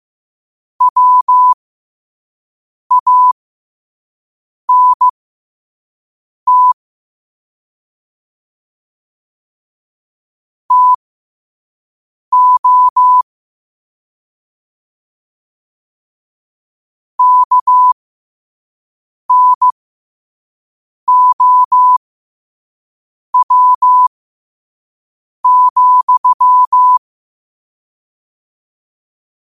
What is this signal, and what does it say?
Morse code, heard as a keyed tone, about 15 wpm, spaced as for 5 wpm, 1000 Hz. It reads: WANT TO KNOW,
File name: qotdfiles / 5wpm